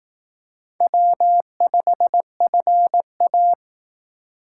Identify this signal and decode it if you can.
Morse code, heard as a keyed tone, about 18 wpm, 700 Hz. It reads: W5FA
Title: Speed Racing - USA 1x2 Call Signs 15wpm